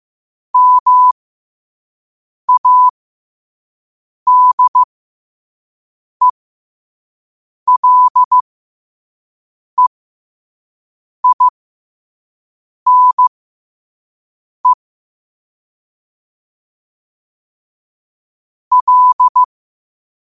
Morse code, heard as a keyed tone, about 15 wpm, spaced as for 5 wpm, 1000 Hz. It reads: MADELEINE L
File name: qotdfiles / 5wpm